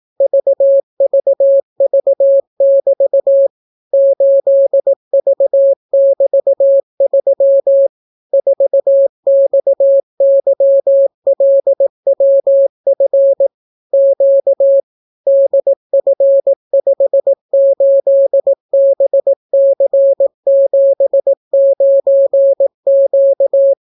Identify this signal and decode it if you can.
Morse code, heard as a keyed tone, about 18 wpm, 550 Hz. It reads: VVV<BT> 8V<BT>3 4XYLWF Q DF58BC79Q